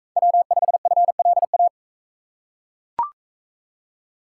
Morse code, heard as a keyed tone, about 39 wpm, 700 Hz. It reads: W5FLA